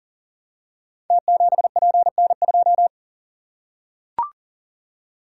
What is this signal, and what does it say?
Morse code, heard as a keyed tone, about 43 wpm, 700 Hz. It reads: T7PN2
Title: Code Groups - 5 Letter:Numbers - 40wpm